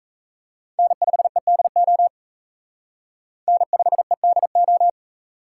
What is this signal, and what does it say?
Morse code, heard as a keyed tone, about 34 wpm, 700 Hz. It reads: N5EDO N5EDO
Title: Speed Racing - USA Call Signs 28wpm